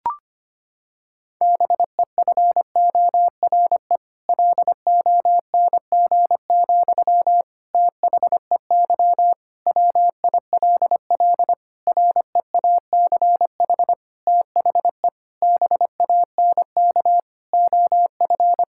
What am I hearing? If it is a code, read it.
Morse code, heard as a keyed tone, about 25 wpm, 700 Hz. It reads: BEFORE LONG, THEY WILL REACH THE BANK OF
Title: Sentences from Top 1000 Words - Review Part 6 - 25wpm